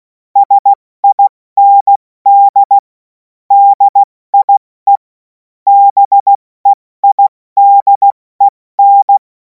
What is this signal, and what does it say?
Morse code, heard as a keyed tone, about 16 wpm, 800 Hz. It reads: SIND DIE BEIDEN
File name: Morsekurs / Witze / Witz 10. März 2026 70 F80